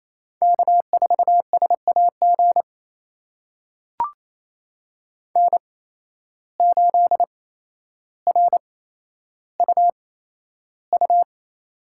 Morse code, heard as a keyed tone, about 28 wpm, spaced as for 12 wpm, 700 Hz. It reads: K4SAG N8RUU
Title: USA Call Signs - 5 Characters - 8x Character Spacing - 28wpm